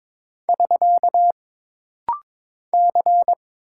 Morse code, heard as a keyed tone, about 22 wpm, 700 Hz. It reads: <SK> C
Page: This rapid-fire timing is geared to push your ICR (Instant Character Recognition) skills.